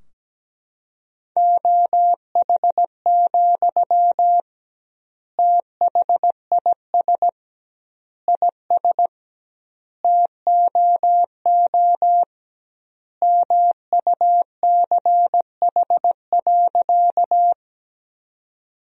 Morse code, heard as a keyed tone, about 17 wpm, 700 Hz. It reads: OH, THIS IS TOO MUCH.
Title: 2x Word-Spacing - Sentences from Top 500 Words - 17wpm